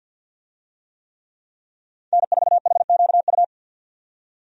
Morse code, heard as a keyed tone, about 52 wpm, 700 Hz. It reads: N4HQU
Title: USA Call Signs - 5 Characters - 50wpm